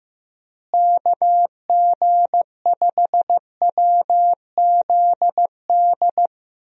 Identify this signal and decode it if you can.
Morse code, heard as a keyed tone, about 15 wpm, 700 Hz. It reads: KG5WZD
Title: USA Call Signs - 6 Characters - Repeated 2x - 15wpm